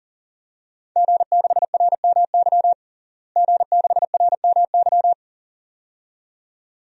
Morse code, heard as a keyed tone, about 40 wpm, 700 Hz. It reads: G6RMY G6RMY
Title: DXCC-100 Call Signs - 5 Characters - Repeated 2x - 40wpm